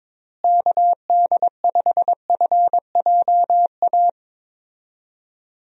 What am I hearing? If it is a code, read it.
Morse code, heard as a keyed tone, about 22 wpm, 700 Hz. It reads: KD5FJA
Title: USA Call Signs - 6 Characters - Repeated 2x - 22wpm